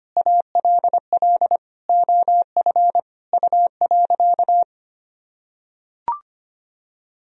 Morse code, heard as a keyed tone, about 25 wpm, 700 Hz. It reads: ALL OF U.
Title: Sentences from Top 500 Words and Abbreviations 25wpm